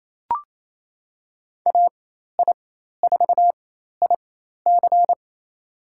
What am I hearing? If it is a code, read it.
Morse code, heard as a keyed tone, about 28 wpm, spaced as for 13 wpm, 700 Hz. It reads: AI4IC
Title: USA Call Signs - 5 Characters - 4x Character Spacing - 28wpm